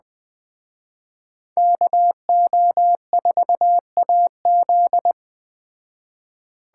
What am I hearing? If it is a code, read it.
Morse code, heard as a keyed tone, about 20 wpm, 700 Hz. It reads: KO4AZ